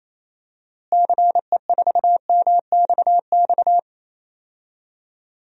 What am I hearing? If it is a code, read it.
Morse code, heard as a keyed tone, about 28 wpm, 700 Hz. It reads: CE4MXX
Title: DXCC-100 Call Signs - 6 Characters - Repeated 2x - 28wpm